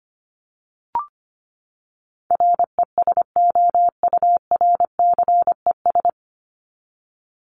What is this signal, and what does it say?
Morse code, heard as a keyed tone, about 25 wpm, 700 Hz. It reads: RESOURCES
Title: ES:S Suffix - Top 100 Words with Base Words 25wpm